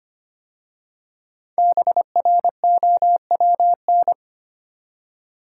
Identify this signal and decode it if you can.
Morse code, heard as a keyed tone, about 25 wpm, 700 Hz. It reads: BROWN
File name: 5 Letter Words - First 3 Letters Given - 25wpm